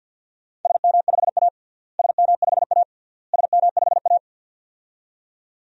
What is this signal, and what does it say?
Morse code, heard as a keyed tone, about 51 wpm, 700 Hz. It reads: SM5A SM5A SM5A
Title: DXCC-100 Call Signs - 4 Characters - Repeated 3x - 50wpm